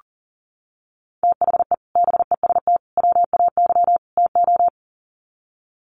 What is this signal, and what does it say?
Morse code, heard as a keyed tone, about 40 wpm, 700 Hz. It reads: THE BEST WAY TO
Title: Sets of 4 Words 40wpm